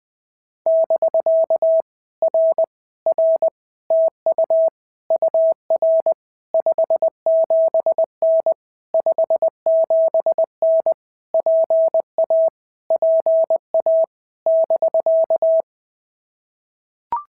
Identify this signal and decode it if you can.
Morse code, heard as a keyed tone, about 20 wpm, 650 Hz. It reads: <BK> R R TU UR 57N 57N PA PA <BK>
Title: POTA QSO Practice - 20wpm